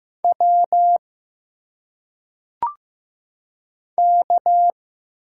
Morse code, heard as a keyed tone, about 15 wpm, 700 Hz. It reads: W K
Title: Single Letters - Phonetic Pronunciation 15wpm